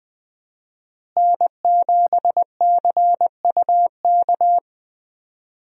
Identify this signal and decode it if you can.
Morse code, heard as a keyed tone, about 20 wpm, 700 Hz. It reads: N7CUK
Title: USA Call Signs - 5 Characters - Repeated 2x - 20wpm